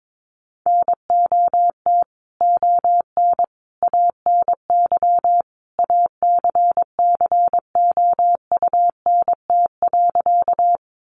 Morse code, heard as a keyed tone, about 22 wpm, 700 Hz. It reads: NOT ON ANY ACCOUNT.
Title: Sentences from Top 900 Words 22wpm